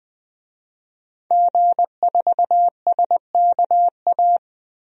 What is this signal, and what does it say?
Morse code, heard as a keyed tone, about 20 wpm, 700 Hz. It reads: G4SKA
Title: DXCC-100 Call Signs - 5 Characters - 20wpm